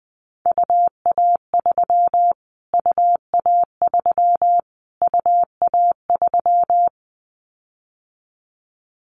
Morse code, heard as a keyed tone, about 20 wpm, 700 Hz. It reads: UA3 UA3 UA3
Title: Callsign Prefixes for Top 25 DXCC - Repeated 3x - 20wpm